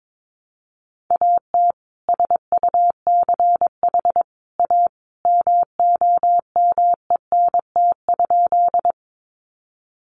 Morse code, heard as a keyed tone, about 22 wpm, 700 Hz. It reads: AT SUCH A MOMENT?
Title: Sentences from Top 1000 Words 22wpm